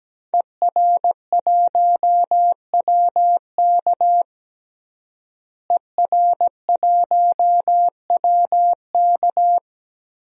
Morse code, heard as a keyed tone, about 17 wpm, 700 Hz. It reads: ER1WK ER1WK
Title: DXCC-100 Call Signs - 5 Characters - Repeated 3x - 17wpm